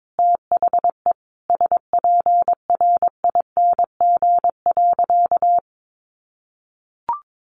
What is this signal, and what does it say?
Morse code, heard as a keyed tone, about 22 wpm, 700 Hz. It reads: THE SPRING.
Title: Sentences from Top 1200 Words 22wpm